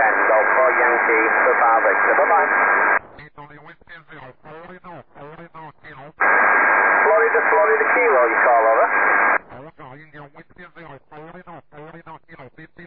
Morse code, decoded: none